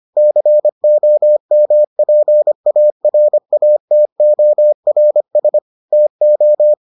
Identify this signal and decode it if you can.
Morse code, heard as a keyed tone, about 25 wpm, 600 Hz. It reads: COMPARATORS TO